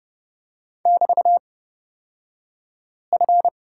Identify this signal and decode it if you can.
Morse code, heard as a keyed tone, about 31 wpm, spaced as for 5 wpm, 700 Hz. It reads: <BT>F
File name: Lesson 36d - BT - Two Characters 30wpm